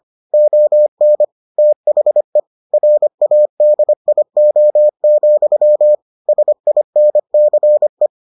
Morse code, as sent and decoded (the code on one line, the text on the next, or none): --- -. / - .... / .-. .- -.. .. --- --..-- / ... .. -. -.-. .
ON THE RADIO, SINCE